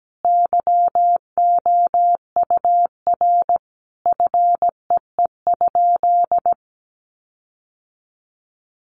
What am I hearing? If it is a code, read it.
Morse code, heard as a keyed tone, about 17 wpm, 700 Hz. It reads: YOUR FEE?
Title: Sentences from Top 1500 Words 17wpm